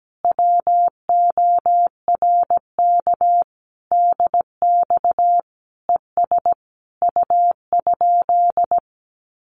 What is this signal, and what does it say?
Morse code, heard as a keyed tone, about 17 wpm, 700 Hz. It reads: WORK DX ES U?
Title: Common QSO Phrases Encore 17wpm